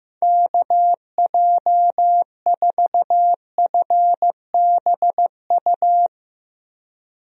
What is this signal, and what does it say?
Morse code, heard as a keyed tone, about 16 wpm, 700 Hz. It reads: KJ4FBU
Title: USA Call Signs - 6 Characters - Repeated 2x - 15wpm